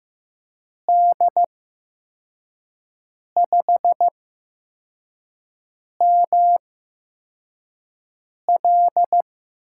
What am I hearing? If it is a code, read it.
Morse code, heard as a keyed tone, about 15 wpm, spaced as for 4 wpm, 700 Hz. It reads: D5ML
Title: USA Call Signs - 8x Character Spacing - 15wpm